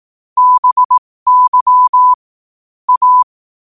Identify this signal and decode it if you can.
Morse code, heard as a keyed tone, about 18 wpm, spaced as for 15 wpm, 1000 Hz. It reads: BY A
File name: qotdfiles / 15wpm